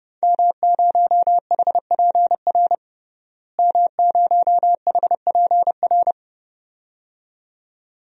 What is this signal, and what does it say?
Morse code, heard as a keyed tone, about 30 wpm, 700 Hz. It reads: M0HPR M0HPR
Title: DXCC-100 Call Signs - 5 Characters - Repeated 2x - 30wpm